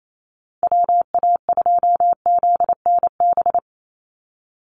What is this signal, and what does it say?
Morse code, heard as a keyed tone, about 28 wpm, 700 Hz. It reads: WA2ZNB
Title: USA Call Signs 28wpm